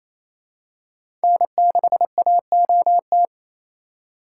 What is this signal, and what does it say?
Morse code, heard as a keyed tone, about 28 wpm, 700 Hz. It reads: N6AOT
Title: USA Call Signs - 5 Characters - 2x Character Spacing - 28wpm